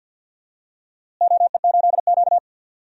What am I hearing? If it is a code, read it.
Morse code, heard as a keyed tone, about 46 wpm, 700 Hz. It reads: OE8Q
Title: DXCC-100 Call Signs - 4 Characters - Repeated 3x - 50wpm